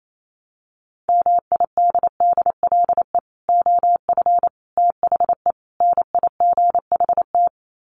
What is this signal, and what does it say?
Morse code, heard as a keyed tone, about 28 wpm, 700 Hz. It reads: MIDDLE OF THE NIGHT
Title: Sets of 4 Words 28wpm